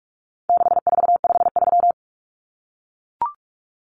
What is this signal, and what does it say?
Morse code, heard as a keyed tone, about 44 wpm, 700 Hz. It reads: 6453
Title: Numbers - 4-Digit 45wpm